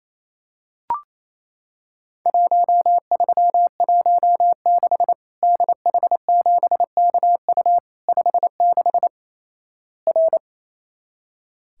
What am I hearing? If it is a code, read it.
Morse code, heard as a keyed tone, about 28 wpm, 700 Hz. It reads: 1316 DH7KU 56 R
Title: Worked All Europe DX Contest - QTC Exchange - 28wpm